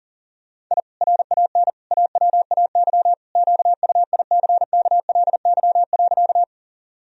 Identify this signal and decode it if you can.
Morse code, heard as a keyed tone, about 40 wpm, 700 Hz. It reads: I RAN AWAY QUICKLY.
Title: Sentences from Top 1000 Words - Review Part 3 - 40wpm